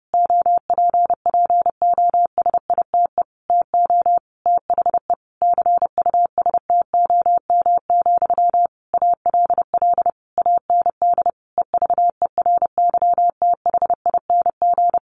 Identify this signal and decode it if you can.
Morse code, heard as a keyed tone, about 30 wpm, 700 Hz. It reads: OPPOSITE TO THE CUSTOM, ALL AND EVERYTHING
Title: A Sound Of Thunder 30wpm